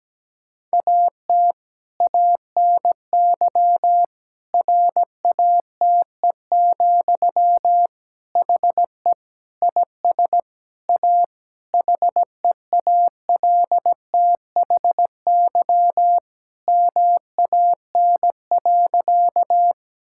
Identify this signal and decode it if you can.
Morse code, heard as a keyed tone, about 17 wpm, 700 Hz. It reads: AT ANY RATE, HE IS A HEALTHY MAN.
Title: Sentences from Top 1000 Words - Review Part 5 - 17wpm